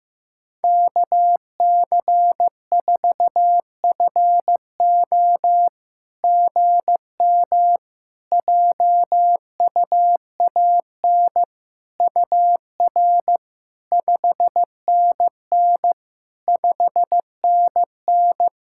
Morse code, heard as a keyed tone, about 15 wpm, 700 Hz. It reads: KC4FO GM JUAN UR 5NN 5NN
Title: SOTA QSO Practice 15wpm